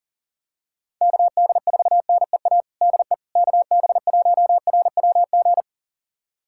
Morse code, heard as a keyed tone, about 40 wpm, 700 Hz. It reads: KD4NEA DE KB1RWG
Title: USA Callsign DE Callsign - 6 Characters - 40wpm